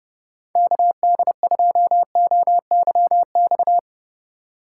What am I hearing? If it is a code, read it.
Morse code, heard as a keyed tone, about 35 wpm, 700 Hz. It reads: KD2OYX
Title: Speed Racing - USA Call Signs 30wpm